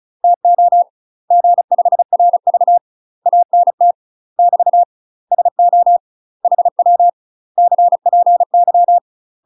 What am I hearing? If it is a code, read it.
Morse code, heard as a keyed tone, about 35 wpm, 700 Hz. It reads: TO G5RV ANT <BT> SO HW CPY
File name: UK MASTER QSOs TRAINING / 35WPM / UK MASTER QSOs 0016